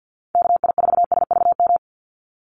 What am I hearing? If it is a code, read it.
Morse code, heard as a keyed tone, about 50 wpm, 700 Hz. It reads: KI4SVM
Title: NA SOTA Activator Callsigns 50wpm